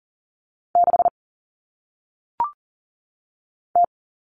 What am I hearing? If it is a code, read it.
Morse code, heard as a keyed tone, about 40 wpm, 700 Hz. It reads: 6 T
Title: Common Cut Numbers - 1-Digit 40wpm